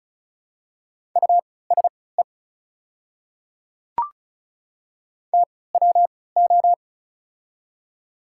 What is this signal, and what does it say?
Morse code, heard as a keyed tone, about 36 wpm, spaced as for 20 wpm, 700 Hz. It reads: USE TWO
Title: 3x Character Spacing - Top 100 Words - 35wpm